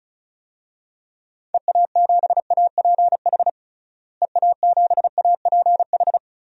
Morse code, heard as a keyed tone, about 34 wpm, 700 Hz. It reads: EA7APH EA7APH
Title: DXCC-100 Call Signs - 6 Characters - Repeated 3x - 35wpm